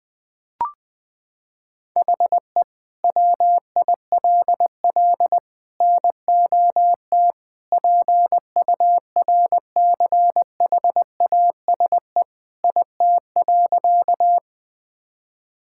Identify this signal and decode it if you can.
Morse code, heard as a keyed tone, about 20 wpm, 700 Hz. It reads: HE WILL NOT PURCHASE IT.
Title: Sentences from Top 1000 Words - Review Part 5 - 20wpm